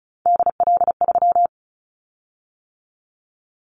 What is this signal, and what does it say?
Morse code, heard as a keyed tone, about 36 wpm, 700 Hz. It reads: DL3
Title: Callsign Prefixes for Top 25 DXCC - 35wpm